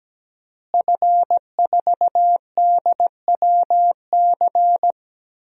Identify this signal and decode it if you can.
Morse code, heard as a keyed tone, about 17 wpm, 700 Hz. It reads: F4DWC
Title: DXCC-100 Call Signs - 5 Characters - Repeated 2x - 17wpm